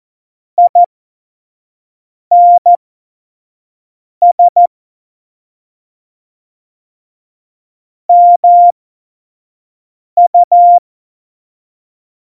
Morse code, heard as a keyed tone, about 14 wpm, spaced as for 5 wpm, 700 Hz. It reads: INS MU
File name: Morsekurs / Witze / Witz 04. September 2018, 25 F70